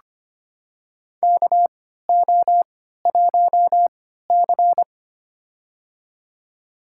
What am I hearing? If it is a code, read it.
Morse code, heard as a keyed tone, about 25 wpm, spaced as for 14 wpm, 700 Hz. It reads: KO1C
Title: USA Call Signs - 4 Characters - 3x Character Spacing - 25wpm